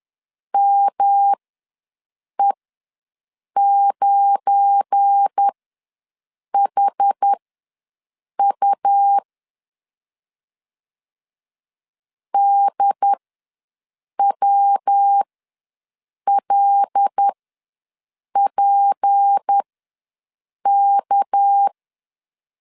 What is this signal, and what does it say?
Morse code, heard as a keyed tone, about 10 wpm, spaced as for 6 wpm, 800 Hz. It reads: ME9HU DWLPK
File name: LEZIONI / 7 GF9Z / 7 GF9 ws 10wpm 4034